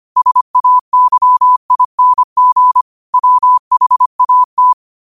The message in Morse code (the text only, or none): SAYING WHAT